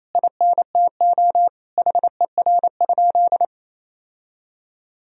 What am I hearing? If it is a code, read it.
Morse code, heard as a keyed tone, about 28 wpm, 700 Hz. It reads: INTO HER?
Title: Sentences from Top 1000 Words - Review Part 1 - 28wpm